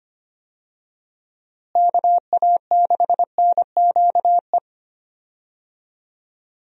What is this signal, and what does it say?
Morse code, heard as a keyed tone, about 25 wpm, 700 Hz. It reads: KA6NQE
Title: USA Call Signs - 6 Characters - Repeated 3x - 25wpm